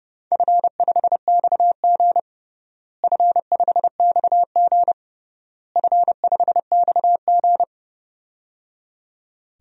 Morse code, heard as a keyed tone, about 30 wpm, 700 Hz. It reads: F5XG F5XG F5XG
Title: DXCC-100 Call Signs - 4 Characters - Repeated 3x - 30wpm